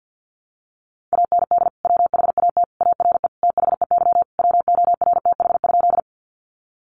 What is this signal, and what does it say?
Morse code, heard as a keyed tone, about 50 wpm, 700 Hz. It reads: AND WHAT ARE THEY WORTH?
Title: Sentences from Top 800 Words 50wpm